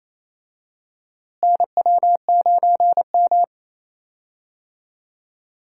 Morse code, heard as a keyed tone, about 28 wpm, 700 Hz. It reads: NW9M
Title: USA Call Signs - 4 Characters - Repeated 2x - 28wpm